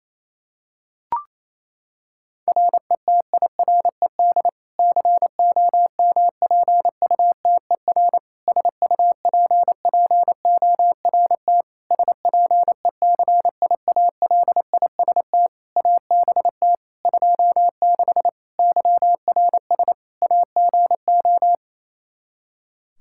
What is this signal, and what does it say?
Morse code, heard as a keyed tone, about 28 wpm, 700 Hz. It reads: RETIRED COMPUTER SUPPORT SPECIALIST ABT 26 YRS AGO
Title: Practice QSO - Exchange 3 - More Info - 28wpm